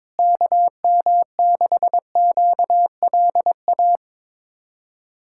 Morse code, heard as a keyed tone, about 22 wpm, 700 Hz. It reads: KM6QLA
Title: USA Callsign DE Callsign 22wpm